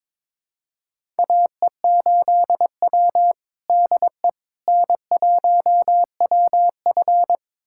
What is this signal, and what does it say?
Morse code, heard as a keyed tone, about 21 wpm, 700 Hz. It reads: AE8W DE N1WF
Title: USA Callsign DE Callsign - 4 Characters - 22wpm